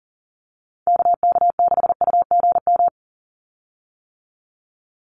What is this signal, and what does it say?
Morse code, heard as a keyed tone, about 40 wpm, 700 Hz. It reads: KK6UGM
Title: USA Call Signs 40wpm